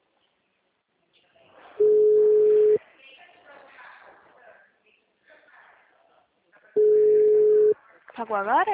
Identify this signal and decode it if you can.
Ringback tone (Europe, 425 Hz), with speech.